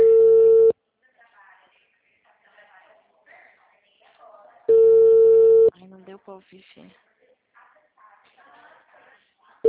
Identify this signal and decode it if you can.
Ringback tone (China, 450 Hz), with speech.
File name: out- -2012- - - .34339